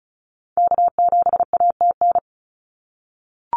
Morse code, heard as a keyed tone, about 34 wpm, 700 Hz. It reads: K7ATN
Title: NA SOTA Activator Callsigns 35wpm